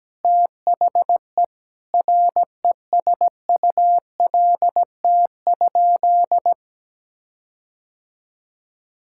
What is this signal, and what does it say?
Morse code, heard as a keyed tone, about 17 wpm, 700 Hz. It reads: THE RESULT?
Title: Sentences from Top 1600 Words 17wpm